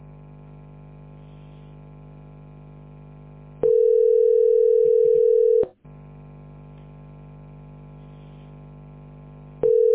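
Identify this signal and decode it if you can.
Ringback tone (North America), 440 + 480 Hz.
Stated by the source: Telephone call recording from the tesspb.mdpbx.fs PBX.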